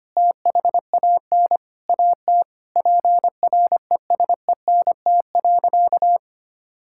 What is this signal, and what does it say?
Morse code, heard as a keyed tone, about 25 wpm, 700 Hz. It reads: THAN AT PRESENT.
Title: Sentences from Top 1000 Words - Review Part 5 - 25wpm